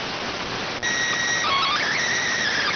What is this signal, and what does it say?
SSTV (slow-scan television), PD 290.